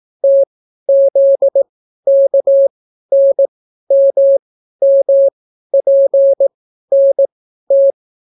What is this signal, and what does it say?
Morse code, heard as a keyed tone, about 18 wpm, spaced as for 12 wpm, 550 Hz. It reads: TZKNMMPNT